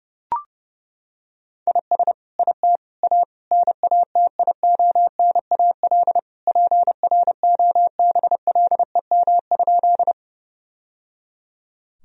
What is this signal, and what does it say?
Morse code, heard as a keyed tone, about 30 wpm, 700 Hz. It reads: IS IT A NATIONAL PROBLEM?
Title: Sentences from Top 1000 Words - Review Part 4 - 30wpm